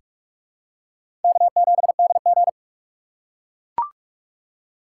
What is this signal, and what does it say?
Morse code, heard as a keyed tone, about 42 wpm, 700 Hz. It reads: K7DG
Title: USA Call Signs - 4 Characters - 6x Character Spacing - 45wpm